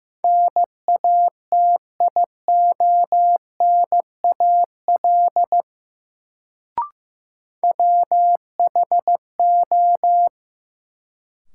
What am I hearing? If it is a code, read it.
Morse code, heard as a keyed tone, about 15 wpm, 700 Hz. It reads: NATIONAL WHO
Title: Sets of 2 Words - New Word Drill - 15wpm